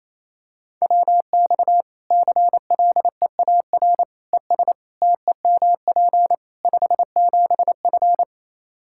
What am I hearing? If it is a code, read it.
Morse code, heard as a keyed tone, about 28 wpm, 700 Hz. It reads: WX CLEAR ES TEMP 57F
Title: Practice QSO - Exchange 2 - Info - 28wpm